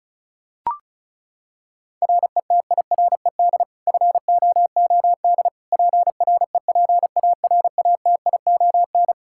Morse code, heard as a keyed tone, about 36 wpm, 700 Hz. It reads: RETIRED FOOD PREPARATION
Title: K1USN SST QSO Practice 35wpm